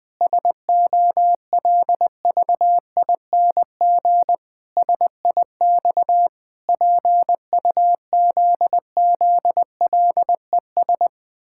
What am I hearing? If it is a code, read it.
Morse code, heard as a keyed tone, about 20 wpm, 700 Hz. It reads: SOLVING SIX PUZZLES
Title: Pangram Challenge v2 - 20wpm